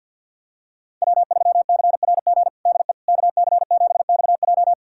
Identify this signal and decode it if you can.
Morse code, heard as a keyed tone, about 46 wpm, 700 Hz. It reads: W3CRG DE KC7XJ
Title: USA Callsign DE Callsign - 5 Characters - 50wpm